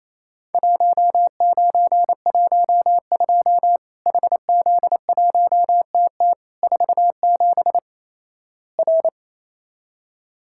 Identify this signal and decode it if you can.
Morse code, heard as a keyed tone, about 27 wpm, 700 Hz. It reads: 1912 HZ1TT 47 R
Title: Worked All Europe DX Contest - QTC Exchange - 28wpm